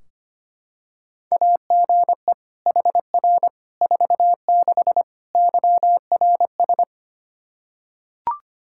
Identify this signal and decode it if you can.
Morse code, heard as a keyed tone, about 25 wpm, 700 Hz. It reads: AGE HR 46 YRS
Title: Practice QSO - Exchange 3 - More Info - 25wpm